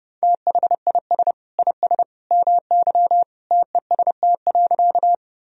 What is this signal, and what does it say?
Morse code, heard as a keyed tone, about 32 wpm, 700 Hz. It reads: THIS IS MY TEST.
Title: Sentences from Top 1000 Words - Review Part 3 - 30wpm